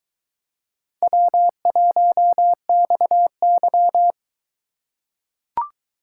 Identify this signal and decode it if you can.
Morse code, heard as a keyed tone, about 22 wpm, 700 Hz. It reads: W1XY